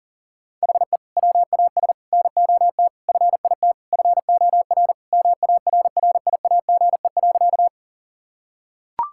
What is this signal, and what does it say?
Morse code, heard as a keyed tone, about 40 wpm, 700 Hz. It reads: HE WAS NOT FIT FOR MARRIAGE.
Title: Sentences from Top 1000 Words - Review Part 5 - 40wpm